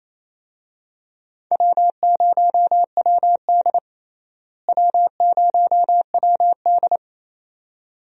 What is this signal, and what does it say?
Morse code, heard as a keyed tone, about 27 wpm, 700 Hz. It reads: W0WD W0WD
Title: USA Call Signs - 4 Characters - Repeated 2x - 28wpm